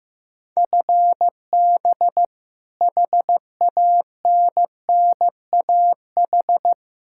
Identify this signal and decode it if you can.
Morse code, heard as a keyed tone, about 16 wpm, 700 Hz. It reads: FB HANNAH
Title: Practice QSO - Exchange 3 - More Info - 15wpm